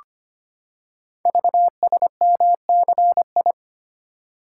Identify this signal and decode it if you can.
Morse code, heard as a keyed tone, about 25 wpm, 700 Hz. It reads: VSMCI
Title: Code Groups - 5 Letter:Numbers - 25wpm